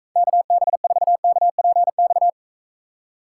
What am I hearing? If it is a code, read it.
Morse code, heard as a keyed tone, about 43 wpm, 700 Hz. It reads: KB4KPX